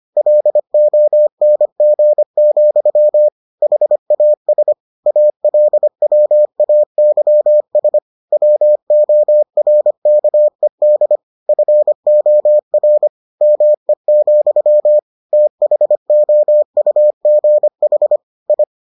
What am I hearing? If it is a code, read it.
Morse code, heard as a keyed tone, about 25 wpm, 600 Hz. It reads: LONG, HAS ALWAYS WORKED FOR ME, THOUGH I